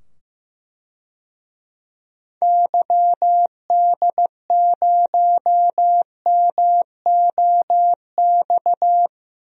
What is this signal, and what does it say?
Morse code, heard as a keyed tone, about 15 wpm, 700 Hz. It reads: YD0MOX
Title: DXCC-100 Call Signs - 6 Characters - 15wpm